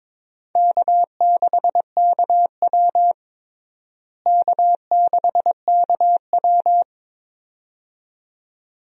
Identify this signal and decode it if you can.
Morse code, heard as a keyed tone, about 22 wpm, 700 Hz. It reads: K6KW K6KW
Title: USA Call Signs - 4 Characters - Repeated 2x - 22wpm